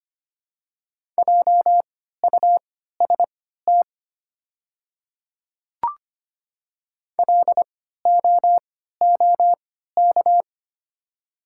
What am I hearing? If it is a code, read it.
Morse code, heard as a keyed tone, about 25 wpm, spaced as for 14 wpm, 700 Hz. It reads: JUST LOOK
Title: 3x Character Spacing - Top 200 Words - 25wpm